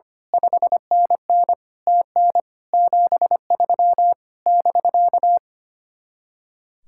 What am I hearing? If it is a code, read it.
Morse code, heard as a keyed tone, about 24 wpm, 700 Hz. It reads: 5NN TN 73 <BK>
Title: Speed-Racing Common QSO Phrases - 25wpm